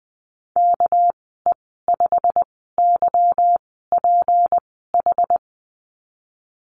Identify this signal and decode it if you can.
Morse code, heard as a keyed tone, about 20 wpm, spaced as for 15 wpm, 700 Hz. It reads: KE5YPH
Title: 200 Lesson 32g - Letter k - Call Signs 20wpm